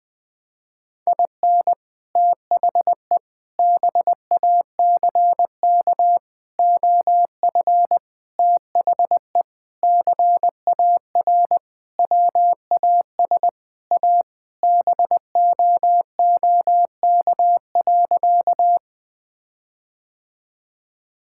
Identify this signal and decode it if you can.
Morse code, heard as a keyed tone, about 20 wpm, 700 Hz. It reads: IN THE BACK OF THE CAR WAS A BOOK.
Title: Sentences with 4-Word Phrases - 20wpm